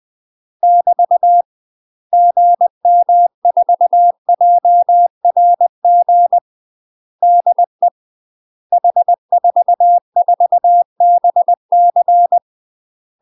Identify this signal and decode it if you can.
Morse code, heard as a keyed tone, about 20 wpm, 700 Hz. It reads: <BT> GM4JRG DE H44BC
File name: UK MASTER QSOs TRAINING / 20WPM / UK MASTER QSOs 0031